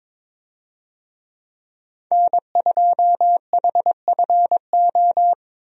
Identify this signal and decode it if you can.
Morse code, heard as a keyed tone, about 22 wpm, 700 Hz. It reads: N2HFO
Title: USA Call Signs - 5 Characters - Repeated 2x - 22wpm